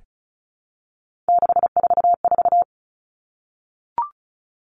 Morse code, heard as a keyed tone, about 34 wpm, 700 Hz. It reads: 644
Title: Numbers - 3-Digit 35wpm